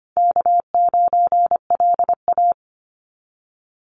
Morse code, heard as a keyed tone, about 25 wpm, 700 Hz. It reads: K9LA
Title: Easy Call Signs - 25wpm